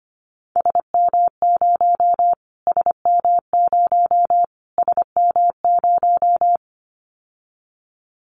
Morse code, heard as a keyed tone, about 25 wpm, 700 Hz. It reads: SM0 SM0 SM0
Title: Callsign Prefixes for Top 25 DXCC - Repeated 3x - 25wpm